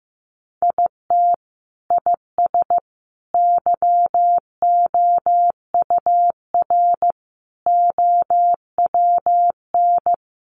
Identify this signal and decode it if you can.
Morse code, heard as a keyed tone, about 15 wpm, 700 Hz. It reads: IT IS YOUR OWN
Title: Sentences from Top 600 Words 15wpm